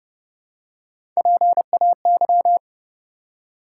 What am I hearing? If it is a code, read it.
Morse code, heard as a keyed tone, about 30 wpm, 700 Hz. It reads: PAY
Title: Three Letter Words - 30wpm